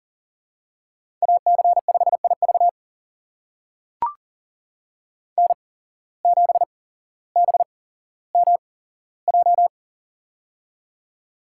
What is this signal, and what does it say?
Morse code, heard as a keyed tone, about 37 wpm, spaced as for 17 wpm, 700 Hz. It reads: AC5IV N7BMJ
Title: USA Call Signs - 5 Characters - 8x Character Spacing - 40wpm